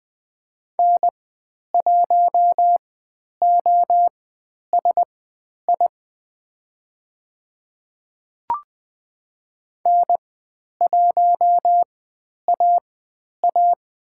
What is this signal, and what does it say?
Morse code, heard as a keyed tone, about 20 wpm, spaced as for 10 wpm, 700 Hz. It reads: N1OSI N1AA
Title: 102 Lesson 17g - Letter u - Call Signs 20wpm